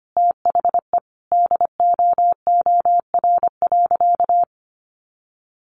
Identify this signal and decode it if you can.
Morse code, heard as a keyed tone, about 25 wpm, 700 Hz. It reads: THE DOOR.
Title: Sentences from Top 400 Words 25wpm